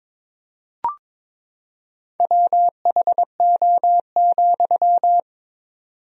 Morse code, heard as a keyed tone, about 21 wpm, 700 Hz. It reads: WHO,
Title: 4x Word-Spacing - Sentences from Top 100 Words - 22wpm